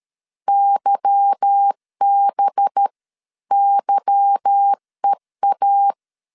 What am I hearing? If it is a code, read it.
Morse code, heard as a keyed tone, about 13 wpm, 800 Hz. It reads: YBYEA
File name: LEZIONI / 8 YJ45 / 8 Y 12wpm 2228